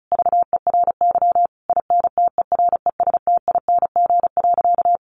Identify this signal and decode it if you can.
Morse code, heard as a keyed tone, about 34 wpm, 700 Hz. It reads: VERY INTERESTING.
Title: Sentences from Top 1000 Words 35wpm